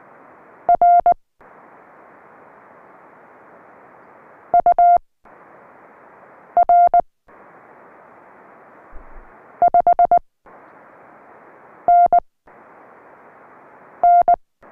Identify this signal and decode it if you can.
Morse code, heard as a keyed tone, about 19 wpm, spaced as for 4 wpm, 700 Hz. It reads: RUR 5NN